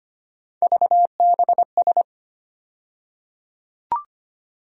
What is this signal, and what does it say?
Morse code, heard as a keyed tone, about 25 wpm, 700 Hz. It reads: VBS
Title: USA Call Sign Postfixes - 3 Letters 25wpm